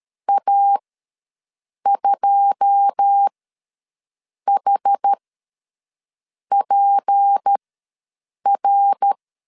Morse code, heard as a keyed tone, about 13 wpm, spaced as for 6 wpm, 800 Hz. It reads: A2HPR
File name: LEZIONI / 9 2 ws 12wpm 3406